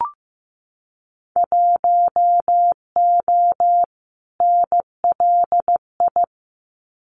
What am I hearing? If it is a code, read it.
Morse code, heard as a keyed tone, about 15 wpm, 700 Hz. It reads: 1O NLI
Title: Winter Field Day Exchanges 15wpm